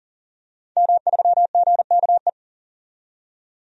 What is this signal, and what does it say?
Morse code, heard as a keyed tone, about 40 wpm, 700 Hz. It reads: M3GKE
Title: DXCC-100 Call Signs - 5 Characters - 40wpm